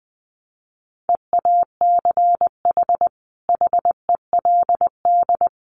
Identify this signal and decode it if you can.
Morse code, heard as a keyed tone, about 20 wpm, 700 Hz. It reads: EACH HELD
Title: A Sound Of Thunder 20wpm